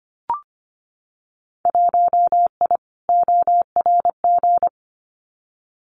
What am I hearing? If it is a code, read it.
Morse code, heard as a keyed tone, about 25 wpm, 700 Hz. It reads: 1I ORG
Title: Winter Field Day Exchanges 25wpm